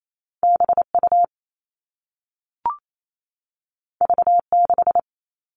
Morse code, heard as a keyed tone, about 28 wpm, 700 Hz. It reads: BU V6
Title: All Cut Numbers - 2-Digit 28wpm